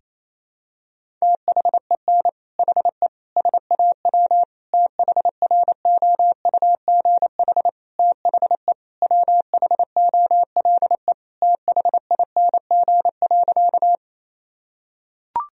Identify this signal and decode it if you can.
Morse code, heard as a keyed tone, about 28 wpm, 700 Hz. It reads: THEN HE SAW THROUGH THE WHOLE THING.
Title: Sentences from Top 1000 Words - Review Part 1 - 28wpm